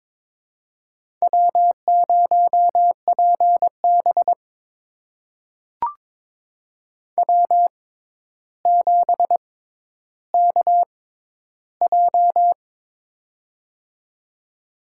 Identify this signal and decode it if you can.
Morse code, heard as a keyed tone, about 22 wpm, spaced as for 11 wpm, 700 Hz. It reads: W0PB W7KJ
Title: USA Call Signs - 6x Character Spacing - 22wpm